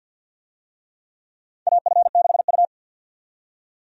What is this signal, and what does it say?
Morse code, heard as a keyed tone, about 52 wpm, 700 Hz. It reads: AF6U
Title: USA Call Signs - 4 Characters - 50wpm